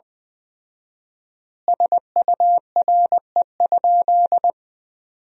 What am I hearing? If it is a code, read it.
Morse code, heard as a keyed tone, about 20 wpm, 700 Hz. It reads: SURE?
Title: 4x Word-Spacing - Sentences from Top 300 Words - 20wpm